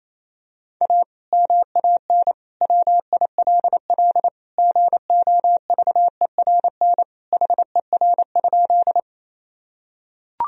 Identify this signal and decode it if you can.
Morse code, heard as a keyed tone, about 28 wpm, 700 Hz. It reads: A MAN WILL GOVERN HER?
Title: Sentences from Top 1000 Words - Review Part 2 - 28wpm